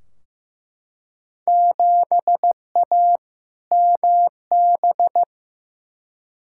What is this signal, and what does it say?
Morse code, heard as a keyed tone, about 15 wpm, 700 Hz. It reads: 7A MB
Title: Field Day Exchanges Encore - 15wpm